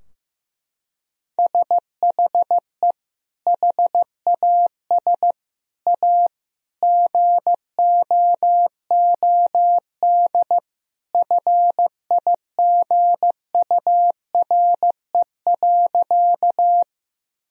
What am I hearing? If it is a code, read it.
Morse code, heard as a keyed tone, about 15 wpm, 700 Hz. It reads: SHE HAS A GOOD FIGURE.
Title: Sentences from Top 1000 Words - Review Part 2 - 15wpm